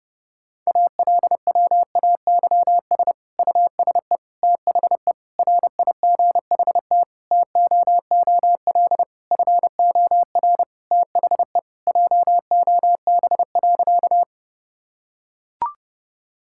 Morse code, heard as a keyed tone, about 32 wpm, 700 Hz. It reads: ALWAYS USE THE RIGHT TOOL FOR THE JOB.
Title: Sentences from Top 1000 Words - Review Part 6 - 30wpm